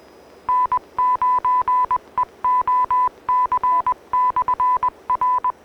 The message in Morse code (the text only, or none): N9EOC/R